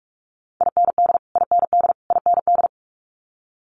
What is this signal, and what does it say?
Morse code, heard as a keyed tone, about 46 wpm, 700 Hz. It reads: IND IND IND
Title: Multi-letter Phonemes Repeated 3x 45wpm